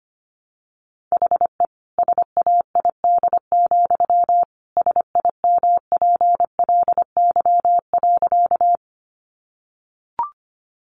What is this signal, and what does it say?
Morse code, heard as a keyed tone, about 25 wpm, 700 Hz. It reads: HE SAID, SIMPLY.
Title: Sentences from Top 600 Words 25wpm